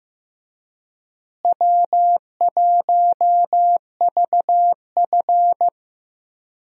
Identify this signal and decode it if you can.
Morse code, heard as a keyed tone, about 16 wpm, 700 Hz. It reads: W1VF
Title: USA Call Signs - 4 Characters - Repeated 3x - 15wpm